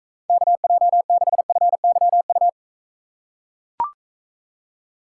Morse code, heard as a keyed tone, about 43 wpm, 700 Hz. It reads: KJ6FYU